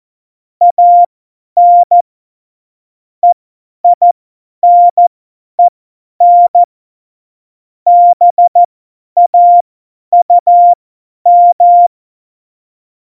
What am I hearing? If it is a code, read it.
Morse code, heard as a keyed tone, about 14 wpm, spaced as for 10 wpm, 700 Hz. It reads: AN EINEN BAUM
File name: Morsekurs / Witze / Witz 04 Juli 17, Tempo50 F70